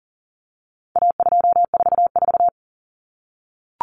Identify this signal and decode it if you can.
Morse code, heard as a keyed tone, about 40 wpm, 700 Hz. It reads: A244
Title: Common Cut Numbers - 4-Digit 40wpm